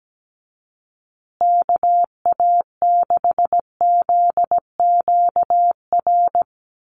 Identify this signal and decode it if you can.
Morse code, heard as a keyed tone, about 17 wpm, 700 Hz. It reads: KA6ZQR
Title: USA Call Signs Repeated 3x 17wpm